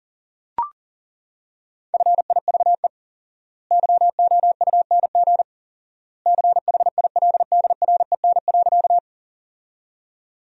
Morse code, heard as a keyed tone, about 37 wpm, 700 Hz. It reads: FIVE YOUNG CHILDREN.
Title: 4x Word-Spacing - Sentences from Top 400 Words - 40wpm